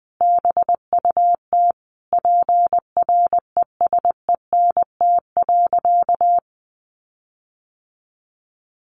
Morse code, heard as a keyed tone, about 20 wpm, 700 Hz. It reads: BUT PRESENT.
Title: Sentences from Top 1300 Words 20wpm